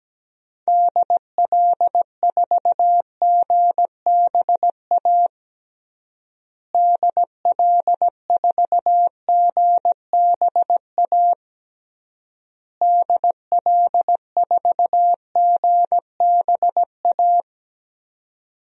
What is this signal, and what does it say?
Morse code, heard as a keyed tone, about 17 wpm, 700 Hz. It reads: DL4GBA DL4GBA DL4GBA
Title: DXCC-100 Call Signs - 6 Characters - Repeated 3x - 17wpm